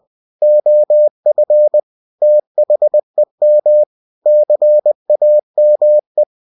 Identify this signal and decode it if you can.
Morse code, heard as a keyed tone, about 20 wpm, 600 Hz. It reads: OF THEM CAME
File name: cw-books / winnie-the-pooh / winnie-20wpm-page-0020